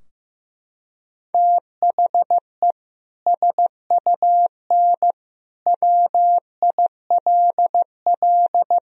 Morse code, heard as a keyed tone, about 15 wpm, 700 Hz. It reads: THE SUN WILL
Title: Sentences from Top 1000 Words - Review Part 3 - 15wpm